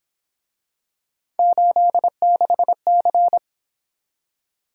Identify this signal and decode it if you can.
Morse code, heard as a keyed tone, about 26 wpm, 700 Hz. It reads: 86C
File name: Speed-Racing - Three Letter-Number Combinations - 22wpm